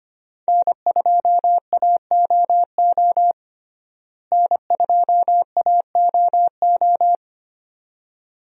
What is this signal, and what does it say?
Morse code, heard as a keyed tone, about 25 wpm, 700 Hz. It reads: N2AOO N2AOO
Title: USA Call Signs - 5 Characters - Repeated 2x - 25wpm